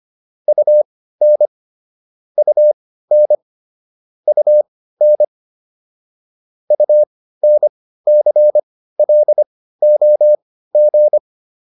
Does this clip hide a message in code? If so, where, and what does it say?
Yes: Morse code from 0.5 to 11.2 s, heard as a keyed tone, about 25 wpm, spaced as for 15 wpm, 600 Hz. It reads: UN UN UN UNCLOG